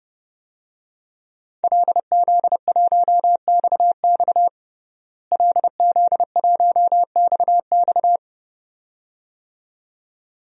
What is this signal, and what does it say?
Morse code, heard as a keyed tone, about 32 wpm, 700 Hz. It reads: LZ1XX LZ1XX
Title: DXCC-100 Call Signs - 5 Characters - Repeated 2x - 30wpm